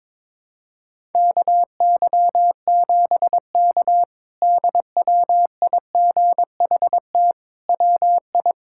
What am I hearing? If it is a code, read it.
Morse code, heard as a keyed tone, about 22 wpm, 700 Hz. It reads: KY7K DWIGHT WI
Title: K1USN SST QSO Practice - 22wpm